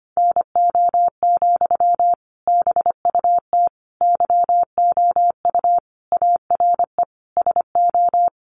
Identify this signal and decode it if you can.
Morse code, heard as a keyed tone, about 25 wpm, 700 Hz. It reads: NO, BUT YOU ARE SO
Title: Sentences from Top 1100 Words 25wpm